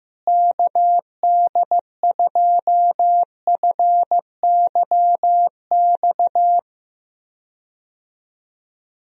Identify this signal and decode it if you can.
Morse code, heard as a keyed tone, about 15 wpm, 700 Hz. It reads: KD2FYX
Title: USA Call Signs - 6 Characters - Repeated 2x - 15wpm